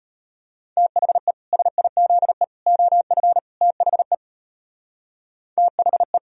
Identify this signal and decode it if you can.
Morse code, heard as a keyed tone, about 33 wpm, 700 Hz. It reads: THE SIZE OF THE THE